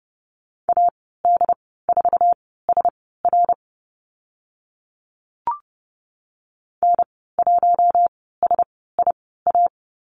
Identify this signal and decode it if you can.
Morse code, heard as a keyed tone, about 30 wpm, spaced as for 17 wpm, 700 Hz. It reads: AD4SR N1SIA
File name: Lesson 15g - Number 5 - Call Signs 30wpm